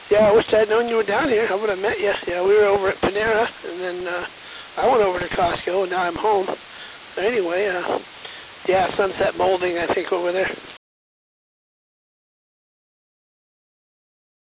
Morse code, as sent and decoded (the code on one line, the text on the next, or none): none